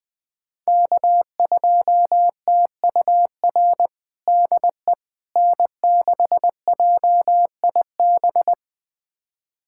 Morse code, heard as a keyed tone, about 20 wpm, 700 Hz. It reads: K2TUR DE N6JIB
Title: USA Callsign DE Callsign - 5 Characters - 20wpm